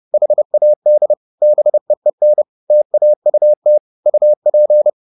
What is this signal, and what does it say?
Morse code, heard as a keyed tone, about 30 wpm, 600 Hz. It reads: HAD BEEN TAUT UP
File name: the old man and the sea / the old man and the sea 1 30wpm